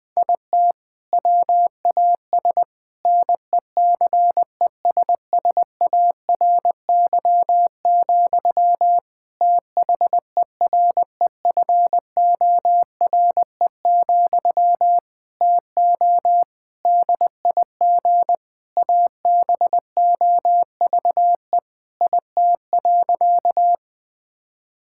Morse code, heard as a keyed tone, about 19 wpm, 700 Hz. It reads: IT WAS NECESSARY, THEREFORE, TO DIG ABOVE IT.
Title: Sentences from Top 1600 Words 20wpm